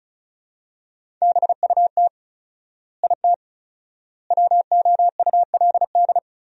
Morse code, heard as a keyed tone, about 34 wpm, 700 Hz. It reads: BUT IT WOULD
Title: 4x Word-Spacing - Sentences from Top 300 Words - 35wpm